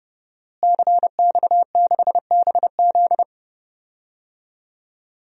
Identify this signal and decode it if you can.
Morse code, heard as a keyed tone, about 31 wpm, 700 Hz. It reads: CX6BZ
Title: DXCC-100 Call Signs - 5 Characters - Repeated 2x - 30wpm